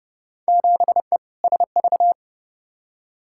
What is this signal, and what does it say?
Morse code, heard as a keyed tone, about 30 wpm, 700 Hz. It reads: 7E SV
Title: Field Day Exchanges Encore - 30wpm